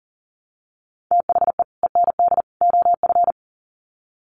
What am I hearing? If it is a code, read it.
Morse code, heard as a keyed tone, about 40 wpm, 700 Hz. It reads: THE END OF
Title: Sets of 3 Words 40wpm